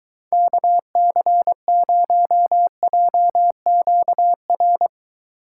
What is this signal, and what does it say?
Morse code, heard as a keyed tone, about 23 wpm, 700 Hz. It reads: KC0JQR